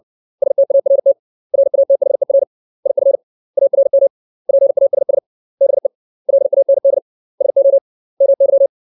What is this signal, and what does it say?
Morse code, heard as a keyed tone, about 60 wpm, 550 Hz. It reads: START RATHER IF ARM PASS BE LAND SO MY
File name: 632 words / 632 Words 60WPM